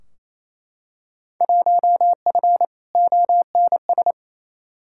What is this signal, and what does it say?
Morse code, heard as a keyed tone, about 28 wpm, 700 Hz. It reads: 1F ONS
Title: Field Day Exchanges Encore - 28wpm